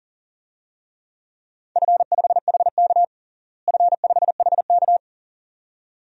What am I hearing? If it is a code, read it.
Morse code, heard as a keyed tone, about 37 wpm, 700 Hz. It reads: F5HK F5HK
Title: DXCC-100 Call Signs - 4 Characters - Repeated 2x - 40wpm